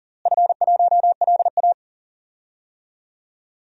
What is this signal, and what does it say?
Morse code, heard as a keyed tone, about 40 wpm, 700 Hz. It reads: F1LA
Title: DXCC-100 Call Signs - 4 Characters - Repeated 3x - 40wpm